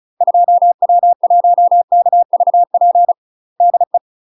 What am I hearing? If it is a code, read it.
Morse code, heard as a keyed tone, about 36 wpm, 700 Hz. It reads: 2W1KVP DE
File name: UK MASTER QSOs TRAINING / 35WPM / UK MASTER QSOs 0009